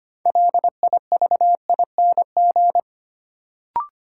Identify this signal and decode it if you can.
Morse code, heard as a keyed tone, about 25 wpm, 700 Hz. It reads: LIVING
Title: Sentences from Top 1000 Words - Review Part 6 - New Word Drill - 25wpm